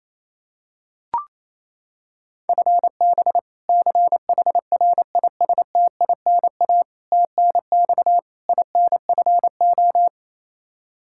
Morse code, heard as a keyed tone, about 28 wpm, 700 Hz. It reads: FB CHRISTINA TNX INFO
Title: Practice QSO - Exchange 3 - More Info - 28wpm